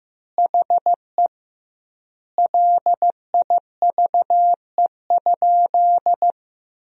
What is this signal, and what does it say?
Morse code, heard as a keyed tone, about 15 wpm, 700 Hz. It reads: HE LIVE?
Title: 2x Word-Spacing - Sentences from Top 200 Words - 15wpm